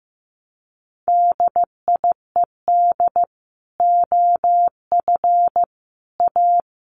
Morse code, heard as a keyed tone, about 15 wpm, 700 Hz. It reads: DIED OF A
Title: Sets of 5 Words Encore2 15wpm